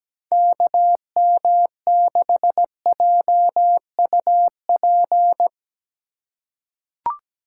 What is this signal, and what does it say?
Morse code, heard as a keyed tone, about 17 wpm, 700 Hz. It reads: KM6JUP
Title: USA Call Signs - 6 Characters - 4x Character Spacing - 17wpm